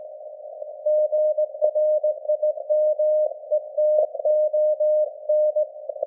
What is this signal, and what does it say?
Morse code, heard as a keyed tone, about 19 wpm, 610 Hz. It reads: GRIMETON
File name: laengstwellenempfang / SDRSharp 17kHz AF